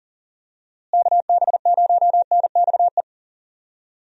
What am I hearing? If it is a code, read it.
Morse code, heard as a keyed tone, about 37 wpm, 700 Hz. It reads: KB0NXE
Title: USA Call Signs - 6 Characters - Repeated 3x - 40wpm